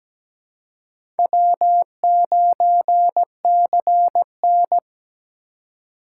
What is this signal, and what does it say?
Morse code, heard as a keyed tone, about 17 wpm, 700 Hz. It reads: W9CN
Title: USA Call Signs - 4 Characters - 2x Character Spacing - 17wpm